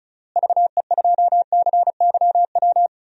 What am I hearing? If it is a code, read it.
Morse code, heard as a keyed tone, about 36 wpm, 700 Hz. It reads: VE2CYW
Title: DXCC-100 Call Signs - 6 Characters - Repeated 3x - 35wpm